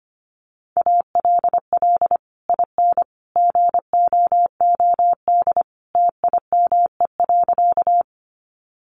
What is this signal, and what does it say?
Morse code, heard as a keyed tone, about 25 wpm, 700 Hz. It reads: ALL IN GOOD TIME.
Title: Sentences from Top 200 Words 25wpm